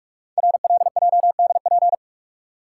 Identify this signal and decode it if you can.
Morse code, heard as a keyed tone, about 44 wpm, 700 Hz. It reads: RLJDP
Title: Code Groups - 5 Letters - 45wpm